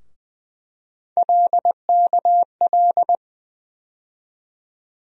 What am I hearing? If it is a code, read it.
Morse code, heard as a keyed tone, about 20 wpm, 700 Hz. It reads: LKL
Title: USA Call Sign Postfixes - 3 Letters 20wpm